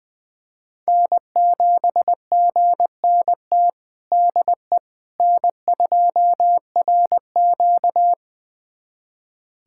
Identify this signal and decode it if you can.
Morse code, heard as a keyed tone, about 21 wpm, 700 Hz. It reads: N7GNT DE N2RQ
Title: USA Callsign DE Callsign 20wpm